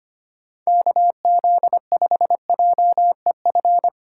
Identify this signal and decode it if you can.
Morse code, heard as a keyed tone, about 25 wpm, 700 Hz. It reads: KZ5JEF
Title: USA Call Signs - 6 Characters - 3x Character Spacing - 25wpm